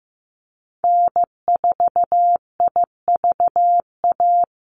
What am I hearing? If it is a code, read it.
Morse code, heard as a keyed tone, about 15 wpm, 700 Hz. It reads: N4IVA
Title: USA Call Signs Repeated 2x 15wpm